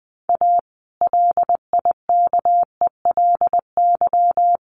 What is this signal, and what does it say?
Morse code, heard as a keyed tone, about 20 wpm, 700 Hz. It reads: A LIKELY
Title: Sentences from Top 600 Words 20wpm